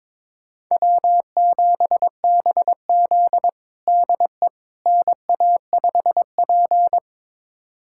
Morse code, heard as a keyed tone, about 22 wpm, 700 Hz. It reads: W7BZ DE NA5P
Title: USA Callsign DE Callsign - 4 Characters - 22wpm